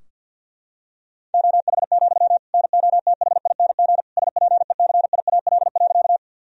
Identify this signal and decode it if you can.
Morse code, heard as a keyed tone, about 52 wpm, 700 Hz. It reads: OH, NOTHING SPECIAL.
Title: Sentences from Top 1000 Words - Review Part 3 - 50wpm